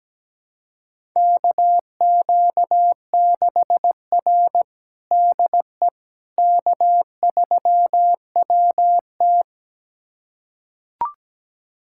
Morse code, heard as a keyed tone, about 17 wpm, 700 Hz. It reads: KQ6R DE K3WT
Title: USA Callsign DE Callsign - 4 Characters - 17wpm